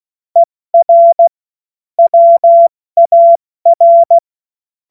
Morse code, heard as a keyed tone, about 16 wpm, 680 Hz. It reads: ER WAR
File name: Morsekurs / Witze / Witz 30. August 2022 70 F80